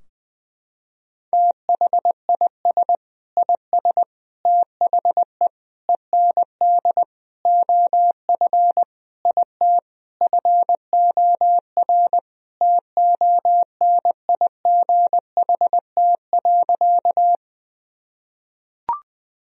Morse code, heard as a keyed tone, about 20 wpm, 700 Hz. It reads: THIS IS THE END OF IT FOR TONIGHT.
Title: Sentences from Top 1000 Words - Review Part 3 - 20wpm